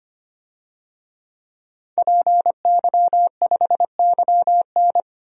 Morse code, heard as a keyed tone, about 25 wpm, 700 Hz. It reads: PY5YN
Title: DXCC-100 Call Signs - 5 Characters - 25wpm